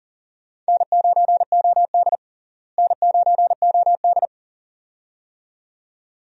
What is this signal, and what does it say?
Morse code, heard as a keyed tone, about 40 wpm, 700 Hz. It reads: N9OD N9OD
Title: USA Call Signs - 4 Characters - Repeated 2x - 40wpm